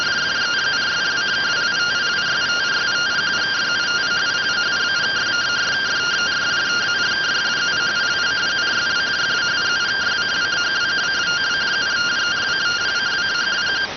Two-tone teleprinter signal, about 170 Hz shift.